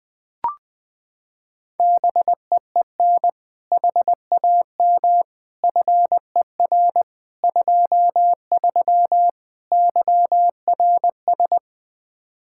Morse code, heard as a keyed tone, about 20 wpm, 700 Hz. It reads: BEEN HAM FER 23 YRS
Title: Practice QSO - Exchange 3 - More Info - 20wpm